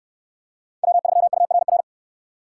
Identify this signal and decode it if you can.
Morse code, heard as a keyed tone, about 69 wpm, 700 Hz. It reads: W3URR